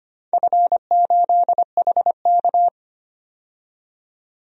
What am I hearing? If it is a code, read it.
Morse code, heard as a keyed tone, about 25 wpm, 700 Hz. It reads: F8HK
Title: DXCC-100 Call Signs - 4 Characters - 25wpm